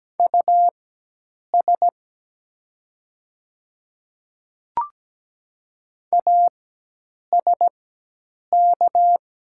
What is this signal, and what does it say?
Morse code, heard as a keyed tone, about 17 wpm, spaced as for 8 wpm, 700 Hz. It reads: US ASK
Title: 4x Character Spacing - Top 300 Words - 17wpm